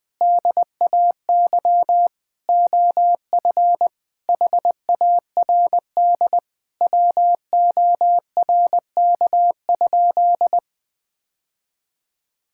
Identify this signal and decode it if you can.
Morse code, heard as a keyed tone, about 20 wpm, 700 Hz. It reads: DAY OF HARD WORK?
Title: Sentences from Top 1000 Words - Review Part 6 - 20wpm